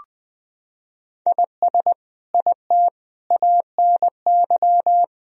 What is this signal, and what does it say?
Morse code, heard as a keyed tone, about 20 wpm, 700 Hz. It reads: IS IT ANY
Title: Sentences from Top 200 Words and Abbreviations 20wpm